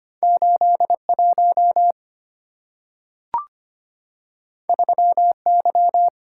Morse code, heard as a keyed tone, about 25 wpm, 690 Hz. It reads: 81 3Y
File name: Two Letter-Number Combinations - 25wpm